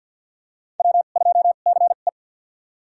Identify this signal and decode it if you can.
Morse code, heard as a keyed tone, about 46 wpm, spaced as for 35 wpm, 700 Hz. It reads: W2CE
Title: USA Call Signs - 4 Characters - 2x Character Spacing - 50wpm